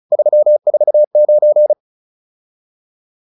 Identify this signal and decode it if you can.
Morse code, heard as a keyed tone, about 36 wpm, 600 Hz. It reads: 349